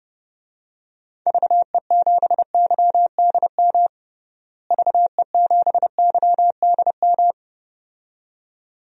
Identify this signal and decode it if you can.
Morse code, heard as a keyed tone, about 30 wpm, 700 Hz. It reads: VE7YDM VE7YDM
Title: DXCC-100 Call Signs - 6 Characters - Repeated 2x - 30wpm